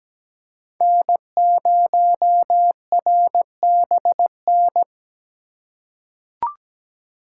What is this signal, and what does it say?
Morse code, heard as a keyed tone, about 17 wpm, 700 Hz. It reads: N0RBN
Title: USA Call Signs - 5 Characters - 2x Character Spacing - 17wpm